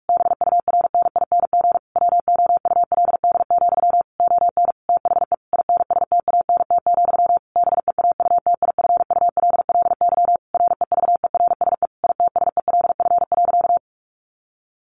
Morse code, heard as a keyed tone, about 44 wpm, 700 Hz. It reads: BURNING WOULD, ON THE INSTANT, BEAUTIFULLY REVERSE ITSELF.
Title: A Sound Of Thunder 45wpm